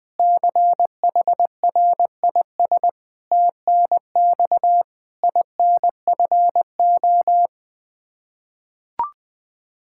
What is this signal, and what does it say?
Morse code, heard as a keyed tone, about 20 wpm, 700 Hz. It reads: CHRIS TNX INFO
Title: Practice QSO - Exchange 3 - More Info - 20wpm